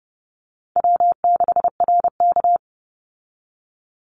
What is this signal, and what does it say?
Morse code, heard as a keyed tone, about 30 wpm, 700 Hz. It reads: W6RK
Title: Call Signs Repeated 2x - 30wpm